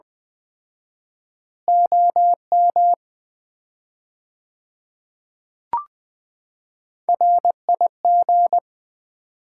Morse code, heard as a keyed tone, about 20 wpm, 700 Hz. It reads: OM RIG
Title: Common QSO Phrases - 4x Word-Spacing - 20wpm